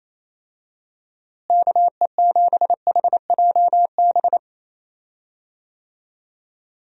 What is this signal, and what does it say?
Morse code, heard as a keyed tone, about 27 wpm, 700 Hz. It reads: KE7HJB
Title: USA Call Signs - 6 Characters - Repeated 2x - 28wpm